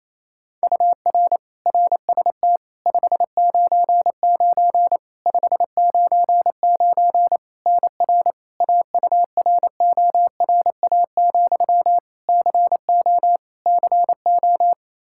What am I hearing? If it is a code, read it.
Morse code, heard as a keyed tone, about 27 wpm, 700 Hz. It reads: UR RST 599 599 NR AURORA, CO CO
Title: K1USN SST QSO Practice 28wpm